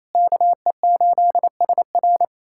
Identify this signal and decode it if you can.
Morse code, heard as a keyed tone, about 28 wpm, 700 Hz. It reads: KE8SR
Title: USA Callsign DE Callsign - 5 Characters - 28wpm